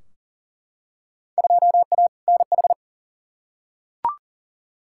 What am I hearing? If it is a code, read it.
Morse code, heard as a keyed tone, about 38 wpm, 700 Hz. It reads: 2A NH
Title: Field Day Exchanges Encore - 40wpm